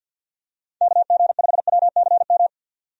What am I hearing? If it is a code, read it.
Morse code, heard as a keyed tone, about 48 wpm, 700 Hz. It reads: KG5WCM
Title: USA Call Signs - 6 Characters - 50wpm